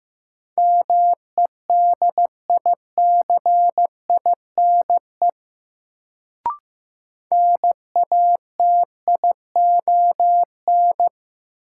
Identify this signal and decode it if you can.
Morse code, heard as a keyed tone, about 16 wpm, 700 Hz. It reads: MEDICINE NATION
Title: Sentences from Top 900 Words - New Word Drill - 15wpm